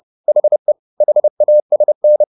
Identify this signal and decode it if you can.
Morse code, heard as a keyed tone, about 30 wpm, 600 Hz. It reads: HE HASN
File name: the old man and the sea / the old man and the sea 1 30wpm